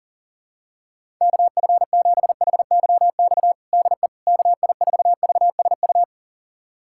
Morse code, heard as a keyed tone, about 40 wpm, 700 Hz. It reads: KF7HYX DE KI4VSU
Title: USA Callsign DE Callsign - 6 Characters - 40wpm